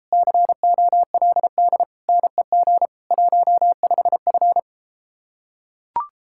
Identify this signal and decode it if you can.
Morse code, heard as a keyed tone, about 33 wpm, 700 Hz. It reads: COLD NEG 15F